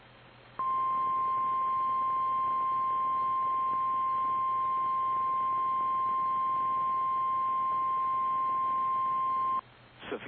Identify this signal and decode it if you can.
Emergency Alert System (SAME) tones (the 1050 Hz attention tone). It is heard from 0.6 to 9.6 s.
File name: eas / KKPRKXPN / KKPRKXPN